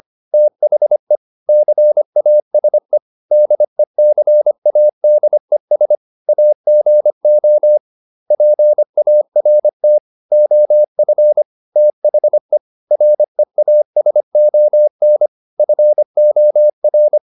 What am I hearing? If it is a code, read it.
Morse code, heard as a keyed tone, about 25 wpm, 600 Hz. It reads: THE CASE DECADES AGO PART OF THE REASON FOR